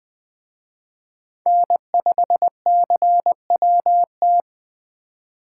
Morse code, heard as a keyed tone, about 20 wpm, 700 Hz. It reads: N5CWT
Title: Speed Racing - USA 1x3 Call Signs 17wpm